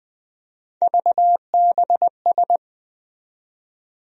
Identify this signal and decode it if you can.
Morse code, heard as a keyed tone, about 20 wpm, 700 Hz. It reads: VBS
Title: USA Call Sign Postfixes - 3 Letters 20wpm